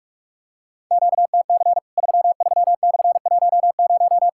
Morse code, heard as a keyed tone, about 46 wpm, 700 Hz. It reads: QTC 33/10
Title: Worked All Europe DX Contest - QTC Exchange - 45wpm